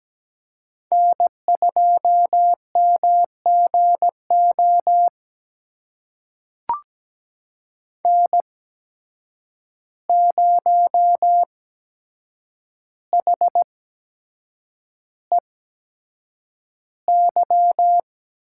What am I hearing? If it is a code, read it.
Morse code, heard as a keyed tone, about 17 wpm, spaced as for 7 wpm, 700 Hz. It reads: N2MGO N0HEY
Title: USA Call Signs - 5 Characters - 8x Character Spacing - 17wpm